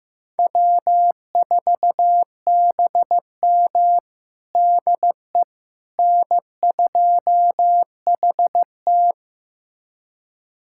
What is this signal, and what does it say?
Morse code, heard as a keyed tone, about 15 wpm, 700 Hz. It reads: W4BM DE N2HT
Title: USA Callsign DE Callsign - 4 Characters - 15wpm